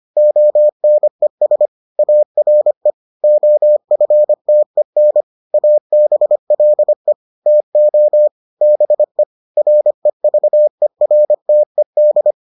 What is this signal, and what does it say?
Morse code, heard as a keyed tone, about 25 wpm, 600 Hz. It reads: ONES ARE OFTEN ABLE TO BE REVERTED